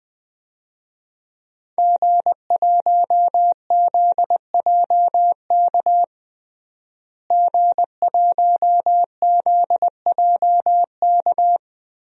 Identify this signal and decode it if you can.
Morse code, heard as a keyed tone, about 20 wpm, 700 Hz. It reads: G1ZJK G1ZJK
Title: DXCC-100 Call Signs - 5 Characters - Repeated 3x - 20wpm